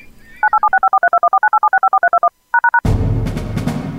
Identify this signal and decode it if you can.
DTMF, heard as telephone touch tones, tones of about 80 ms. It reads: 9876543214987654321#